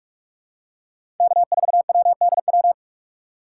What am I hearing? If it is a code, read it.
Morse code, heard as a keyed tone, about 48 wpm, 700 Hz. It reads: K4WDW
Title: USA Call Signs - 5 Characters - 6x Character Spacing - 45wpm